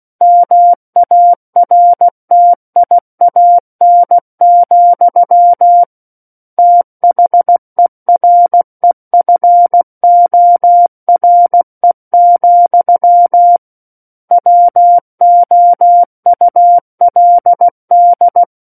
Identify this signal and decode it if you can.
Morse code, heard as a keyed tone, about 16 wpm, 700 Hz. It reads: MARTIAN, THEREFORE, WOULD
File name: Book One / 07-Chapter 7 - 16 WPM